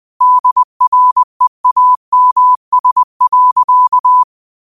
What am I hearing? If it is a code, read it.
Morse code, heard as a keyed tone, about 20 wpm, 1000 Hz. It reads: DREAMS.